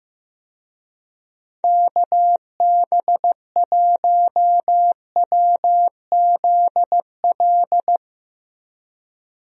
Morse code, heard as a keyed tone, about 15 wpm, 700 Hz. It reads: KB1WZL
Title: USA Call Signs - 6 Characters - Repeated 2x - 15wpm